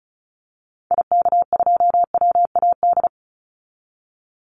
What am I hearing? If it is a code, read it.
Morse code, heard as a keyed tone, about 34 wpm, 700 Hz. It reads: IK2WAD
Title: Call Signs Repeated 2x - 35wpm